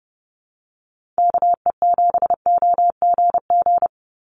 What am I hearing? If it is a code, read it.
Morse code, heard as a keyed tone, about 30 wpm, 700 Hz. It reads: KE7OGG
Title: USA Call Signs 30wpm